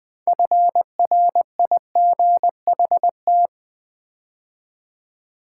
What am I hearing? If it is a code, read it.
Morse code, heard as a keyed tone, about 20 wpm, 700 Hz. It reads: FRIGHT
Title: Build A Word - 2 Letters At A Time - 8-Letter Words - 20wpm